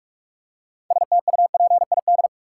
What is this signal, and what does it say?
Morse code, heard as a keyed tone, about 46 wpm, 700 Hz. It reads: STUPID
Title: Six Letter Words - 45wpm